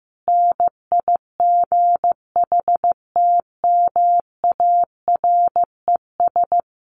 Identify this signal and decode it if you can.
Morse code, heard as a keyed tone, about 15 wpm, 700 Hz. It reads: NIGHTMARES
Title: A Sound Of Thunder 15wpm